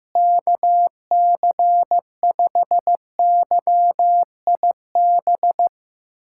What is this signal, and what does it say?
Morse code, heard as a keyed tone, about 15 wpm, 700 Hz. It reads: KC5YIB
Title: USA Callsign DE Callsign - 6 Characters - 15wpm